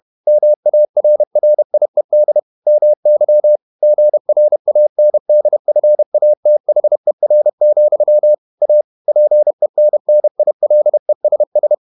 Morse code, heard as a keyed tone, about 31 wpm, 600 Hz. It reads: MARRIED MY GRANDFATHER, A PENNILESS
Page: Never 31wpm | CW med Gnister